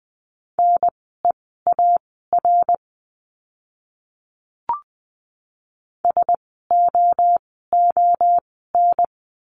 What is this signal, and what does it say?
Morse code, heard as a keyed tone, about 20 wpm, spaced as for 14 wpm, 700 Hz. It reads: NEAR SOON
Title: 054 Lesson 10f - Letter r - Words 20wpm